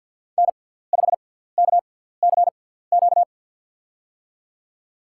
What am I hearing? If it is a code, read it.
Morse code, heard as a keyed tone, about 48 wpm, spaced as for 17 wpm, 700 Hz. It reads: N5KCQ
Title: USA Call Signs - 5 Characters - 6x Character Spacing - 50wpm